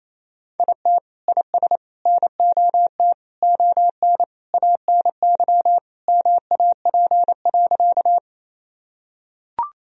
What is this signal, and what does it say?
Morse code, heard as a keyed tone, about 27 wpm, 700 Hz. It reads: IT IS NOT ON ANY MAP.
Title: Sentences from Top 1000 Words - Review Part 2 - 28wpm